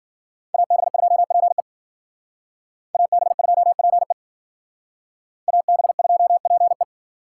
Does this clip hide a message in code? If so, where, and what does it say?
Yes: Morse code from 0.5 to 6.8 s, heard as a keyed tone, about 51 wpm, 700 Hz. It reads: AB2PE AB2PE AB2PE